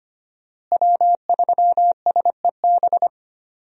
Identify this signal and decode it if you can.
Morse code, heard as a keyed tone, about 25 wpm, 700 Hz. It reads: W3SEB
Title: USA Call Signs - 5 Characters - Repeated 2x - 25wpm